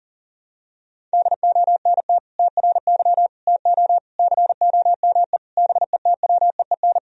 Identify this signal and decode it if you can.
Morse code, heard as a keyed tone, about 40 wpm, 700 Hz. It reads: DONT TRY TO COME BETWEEN
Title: Sentences from Top 1000 Words - Review Part 1 - 40wpm